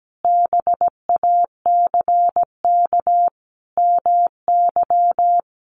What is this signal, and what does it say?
Morse code, heard as a keyed tone, about 17 wpm, 700 Hz. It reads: BACK MY
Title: Sentences from Top 800 Words 17wpm